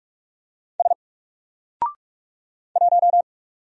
Morse code, heard as a keyed tone, about 46 wpm, 700 Hz. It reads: S 1
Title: Single Letter-Number 45wpm